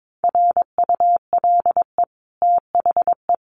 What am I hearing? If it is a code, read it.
Morse code, heard as a keyed tone, about 22 wpm, 700 Hz. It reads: RULE THE
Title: Sentences from Top 600 Words 22wpm